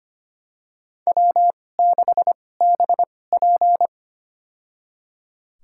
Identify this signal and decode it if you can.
Morse code, heard as a keyed tone, about 25 wpm, spaced as for 18 wpm, 700 Hz. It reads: W6BP
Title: USA Call Signs - 4 Characters - 2x Character Spacing - 25wpm